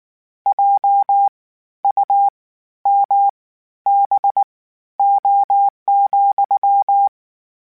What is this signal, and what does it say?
Morse code, heard as a keyed tone, about 19 wpm, spaced as for 12 wpm, 800 Hz. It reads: JUMBO,
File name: Misc / Misc-mp3s / WordPairs sp @18